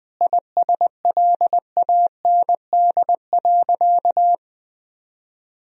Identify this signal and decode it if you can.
Morse code, heard as a keyed tone, about 20 wpm, 700 Hz. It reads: ISLAND.
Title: Sentences from Top 1000 Words - Review Part 3 - 20wpm